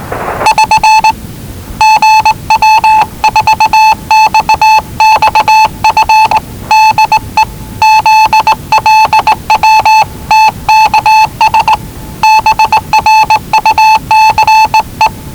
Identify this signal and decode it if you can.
Morse code, heard as a keyed tone, about 19 wpm, 870 Hz. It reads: <SN> GW4XXF DE ZLW TKS BRUCE
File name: WellingtonRadioZlwCloseDownOnHfAndMf / 01-Zlw 4277khz 417.5 qrt until 0925z